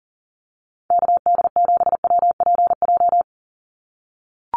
Morse code, heard as a keyed tone, about 40 wpm, 700 Hz. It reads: KD7WPJ
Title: NA SOTA Activator Callsigns 40wpm